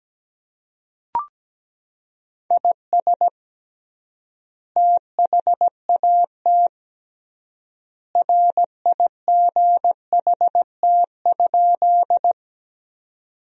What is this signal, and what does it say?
Morse code, heard as a keyed tone, about 17 wpm, 700 Hz. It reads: IS THAT RIGHT?
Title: 3x Word-Spacing - Sentences from Top 200 Words - 17wpm